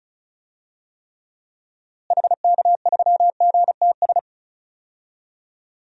Morse code, heard as a keyed tone, about 36 wpm, 700 Hz. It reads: HK3GTS
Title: DXCC-100 Call Signs - 6 Characters - Repeated 3x - 35wpm